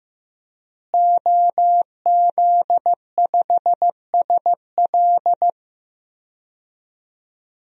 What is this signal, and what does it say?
Morse code, heard as a keyed tone, about 15 wpm, 700 Hz. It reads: OZ5SL
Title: DXCC-100 Call Signs - 5 Characters - Repeated 2x - 15wpm